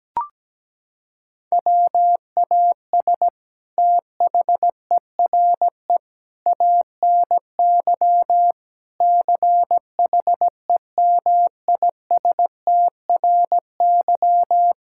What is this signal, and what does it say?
Morse code, heard as a keyed tone, about 17 wpm, 700 Hz. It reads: WAS THERE ANY CHEMISTRY
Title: Sentences from Top 1000 Words - Review Part 6 - 17wpm